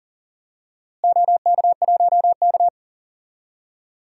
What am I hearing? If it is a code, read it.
Morse code, heard as a keyed tone, about 40 wpm, 700 Hz. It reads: OK1K
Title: DXCC-100 Call Signs - 4 Characters - Repeated 3x - 40wpm